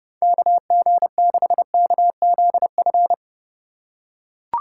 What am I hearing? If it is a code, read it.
Morse code, heard as a keyed tone, about 30 wpm, 700 Hz. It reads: KG6KZF